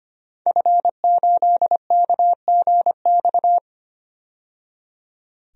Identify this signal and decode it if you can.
Morse code, heard as a keyed tone, about 25 wpm, 700 Hz. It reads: F8KGX
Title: DXCC-100 Call Signs - 5 Characters - 25wpm